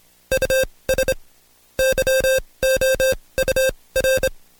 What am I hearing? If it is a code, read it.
Morse code, heard as a keyed tone, about 26 wpm, spaced as for 20 wpm, 530 Hz. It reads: US YOUR
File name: Code Quick / AudioDownld2 / Seg136-Stepper9 19WPM